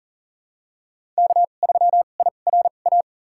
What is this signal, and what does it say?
Morse code, heard as a keyed tone, about 40 wpm, spaced as for 29 wpm, 700 Hz. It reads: K3IRA
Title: USA Call Signs - 5 Characters - 2x Character Spacing - 40wpm